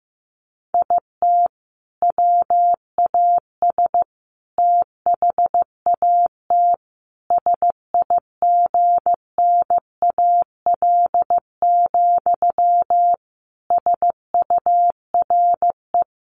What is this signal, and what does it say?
Morse code, heard as a keyed tone, about 15 wpm, 700 Hz. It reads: IT WAS THAT SIGNAL, SURE
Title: Sentences from Top 1400 Words 15wpm